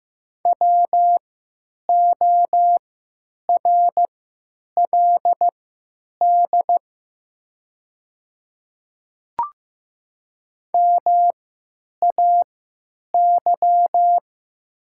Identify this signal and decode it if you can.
Morse code, heard as a keyed tone, about 15 wpm, spaced as for 9 wpm, 700 Hz. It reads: WORLD MAY
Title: 3x Character Spacing - Top 300 Words - 15wpm